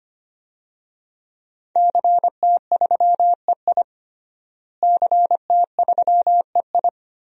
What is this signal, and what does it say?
Morse code, heard as a keyed tone, about 25 wpm, 700 Hz. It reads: CT3EI CT3EI
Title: DXCC-100 Call Signs - 5 Characters - Repeated 2x - 25wpm